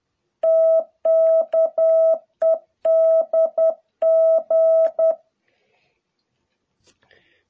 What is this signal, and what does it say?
Morse code, heard as a keyed tone, about 10 wpm, 660 Hz. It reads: TKEDG
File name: jh8cbh / 2021cw